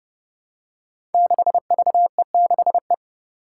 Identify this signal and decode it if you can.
Morse code, heard as a keyed tone, about 30 wpm, 700 Hz. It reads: 6VE6E
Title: Five Challenging Letter-Number Combinations 30wpm